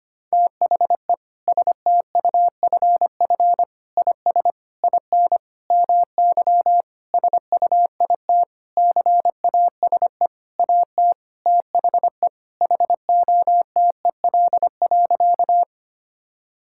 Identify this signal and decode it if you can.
Morse code, heard as a keyed tone, about 25 wpm, 700 Hz. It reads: THE STUFF IS IN MY SUITCASE AT THE HOTEL.
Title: Sentences from Top 1600 Words 25wpm